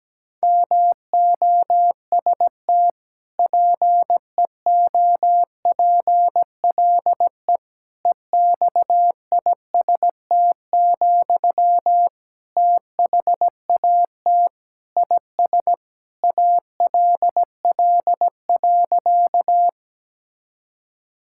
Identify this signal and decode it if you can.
Morse code, heard as a keyed tone, about 17 wpm, 700 Hz. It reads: MOST PEOPLE EXIST, THAT IS ALL.
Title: Sentences from Top 1000 Words - Review Part 5 - 17wpm